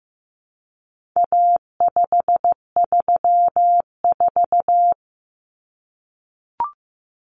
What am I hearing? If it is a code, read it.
Morse code, heard as a keyed tone, about 15 wpm, 700 Hz. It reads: A534
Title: Common Cut Numbers - 4-Digit 15wpm